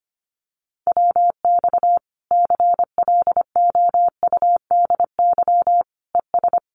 Morse code, heard as a keyed tone, about 25 wpm, 700 Hz. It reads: WX CLOUDY ES
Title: Common QSO Phrases 25wpm